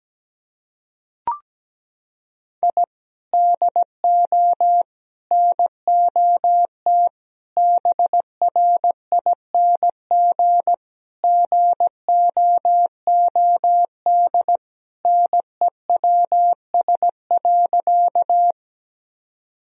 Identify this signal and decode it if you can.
Morse code, heard as a keyed tone, about 17 wpm, 700 Hz. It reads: I DO NOT BRING GOOD NEWS.
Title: Sentences from Top 1000 Words - Review Part 3 - 17wpm